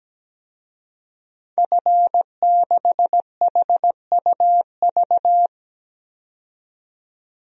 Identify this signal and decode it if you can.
Morse code, heard as a keyed tone, about 17 wpm, 700 Hz. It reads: F6HUV
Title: DXCC-100 Call Signs - 5 Characters - 17wpm